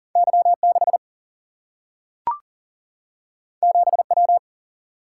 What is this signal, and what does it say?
Morse code, heard as a keyed tone, about 40 wpm, 700 Hz. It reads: Y 6 7 W
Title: Single-Letter Single-Number 40wpm